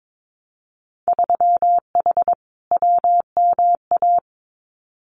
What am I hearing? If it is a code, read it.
Morse code, heard as a keyed tone, about 22 wpm, 700 Hz. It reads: 3H WMA
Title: Winter Field Day Exchanges Encore 22wpm